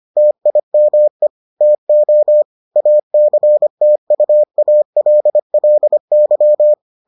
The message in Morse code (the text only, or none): TIME TO ACTUALLY